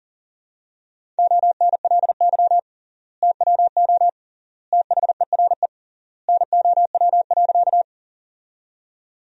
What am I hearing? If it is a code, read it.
Morse code, heard as a keyed tone, about 37 wpm, 700 Hz. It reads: ONLY TWO THERE NOW.
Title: 3x Word-Spacing - Sentences from Top 200 Words - 40wpm